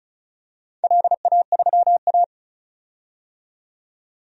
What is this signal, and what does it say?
Morse code, heard as a keyed tone, about 36 wpm, 700 Hz. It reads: LA3A
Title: DXCC-100 Call Signs - 4 Characters - 35wpm